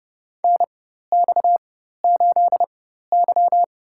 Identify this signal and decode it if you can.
Morse code, heard as a keyed tone, about 30 wpm, spaced as for 14 wpm, 700 Hz. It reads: NX8Y
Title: USA Call Signs - 4 Characters - 4x Character Spacing - 30wpm